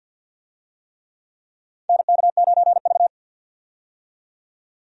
Morse code, heard as a keyed tone, about 52 wpm, 700 Hz. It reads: NK9V
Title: USA Call Signs - 4 Characters - 50wpm